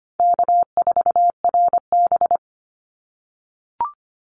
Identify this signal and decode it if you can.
Morse code, heard as a keyed tone, about 25 wpm, 700 Hz. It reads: K4RB
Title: Easy Call Signs - 25wpm